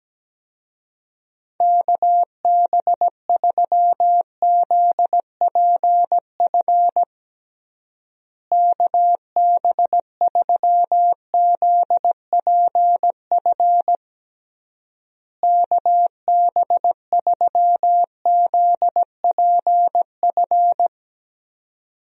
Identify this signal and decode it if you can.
Morse code, heard as a keyed tone, about 17 wpm, 700 Hz. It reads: KB3ZPF KB3ZPF KB3ZPF
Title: USA Call Signs - 6 Characters - Repeated 3x - 17wpm